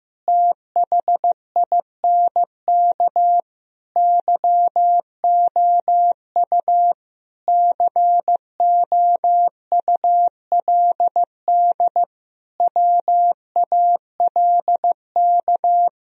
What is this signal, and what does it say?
Morse code, heard as a keyed tone, about 15 wpm, 700 Hz. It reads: THINK YOU COULD WALK
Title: Sentences from Top 1000 Words - Review Part 1 - 15wpm